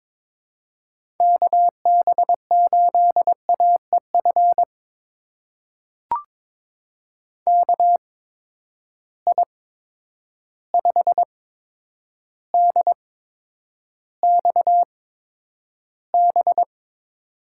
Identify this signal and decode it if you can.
Morse code, heard as a keyed tone, about 22 wpm, spaced as for 9 wpm, 700 Hz. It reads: KB8AEF KI5DXB
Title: USA Call Signs - 6 Characters - 8x Character Spacing - 22wpm